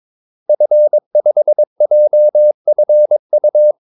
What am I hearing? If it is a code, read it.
Morse code, heard as a keyed tone, about 22 wpm, 600 Hz. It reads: F5JFU
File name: Callsigns-22wpm